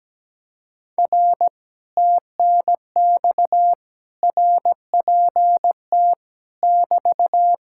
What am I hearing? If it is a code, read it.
Morse code, heard as a keyed tone, about 17 wpm, 700 Hz. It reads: R TNX RPT <BT>
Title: Speed-Racing Common QSO Phrases - 17wpm